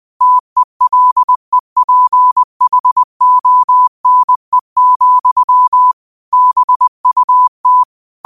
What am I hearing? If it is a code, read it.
Morse code, heard as a keyed tone, about 20 wpm, 1000 Hz. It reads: TELEPHONE, BUT
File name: qotdfiles / 20wpm